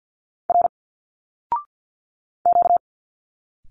Morse code, heard as a keyed tone, about 50 wpm, 700 Hz. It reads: R Q
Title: Single Letters 50wpm